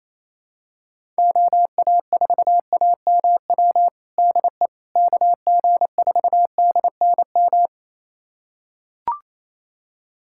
Morse code, heard as a keyed tone, about 28 wpm, 700 Hz. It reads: OA4AMW DE KG4DNM
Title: USA Callsign DE Callsign 28wpm